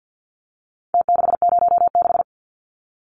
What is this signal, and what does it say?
Morse code, heard as a keyed tone, about 50 wpm, 700 Hz. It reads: T606
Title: Numbers - Contest Style - 4-Digit 50wpm